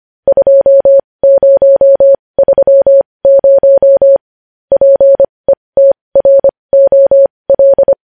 Morse code, heard as a keyed tone, about 25 wpm, spaced as for 20 wpm, 550 Hz. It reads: 2030 PETROL
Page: news headlines in morse code at 20 wpm